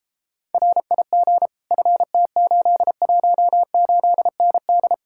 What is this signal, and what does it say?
Morse code, heard as a keyed tone, about 37 wpm, 700 Hz. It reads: RIG FT818ND